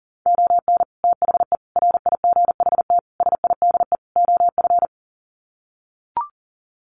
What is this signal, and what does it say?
Morse code, heard as a keyed tone, about 40 wpm, 700 Hz. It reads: ON THE RIGHT SIDE OF
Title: Sets of 5 Words Encore 40wpm